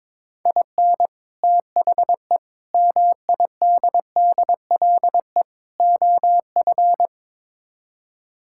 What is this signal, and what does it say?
Morse code, heard as a keyed tone, about 22 wpm, 700 Hz. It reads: IN THE MIDDLE OF
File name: practice / 22wpm / Build A Phrase - 5-Word Phrases - 22wpm